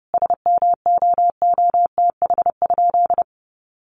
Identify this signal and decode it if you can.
Morse code, heard as a keyed tone, about 30 wpm, 700 Hz. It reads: SMOOTH?
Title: Sentences from Top 1400 Words 30wpm